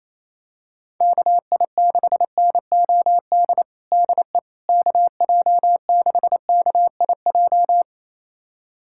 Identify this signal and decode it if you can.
Morse code, heard as a keyed tone, about 28 wpm, 700 Hz. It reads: KI6NOD DE KJ6KIJ
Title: USA Callsign DE Callsign - 6 Characters - 28wpm